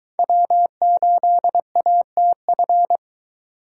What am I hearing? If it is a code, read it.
Morse code, heard as a keyed tone, about 20 wpm, 700 Hz. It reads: W8ATF